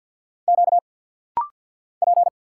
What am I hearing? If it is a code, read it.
Morse code, heard as a keyed tone, about 50 wpm, 690 Hz. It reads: Q P
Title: Single Letters - Rapid-Fire 50wpm